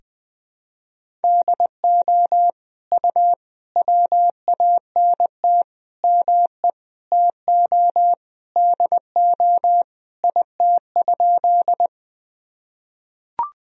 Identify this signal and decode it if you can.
Morse code, heard as a keyed tone, about 21 wpm, 700 Hz. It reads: DO U WANT ME TO DO IT?
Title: Sentences from Top 200 Words and Abbreviations 20wpm